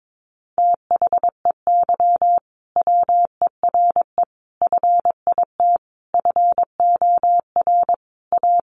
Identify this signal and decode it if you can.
Morse code, heard as a keyed tone, about 22 wpm, 700 Hz. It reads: THEY WERE FIT FOR A
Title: Sentences from Top 800 Words 22wpm